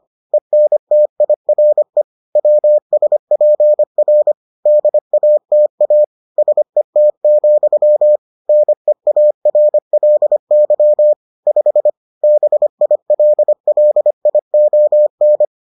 Morse code, heard as a keyed tone, about 25 wpm, 600 Hz. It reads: ENTIRE WSPR DATA SET, NEARLY 5 BILLION